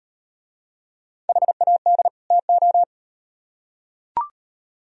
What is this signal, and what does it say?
Morse code, heard as a keyed tone, about 38 wpm, 700 Hz. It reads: HAD TO